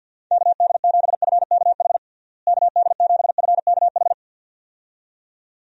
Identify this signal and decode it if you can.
Morse code, heard as a keyed tone, about 51 wpm, 700 Hz. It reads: KD7FKH KD7FKH
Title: USA Call Signs - 6 Characters - Repeated 2x - 50wpm